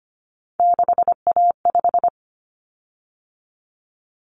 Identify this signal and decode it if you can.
Morse code, heard as a keyed tone, about 25 wpm, 700 Hz. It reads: 6A5
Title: Common Cut Numbers - 3-Digit 25wpm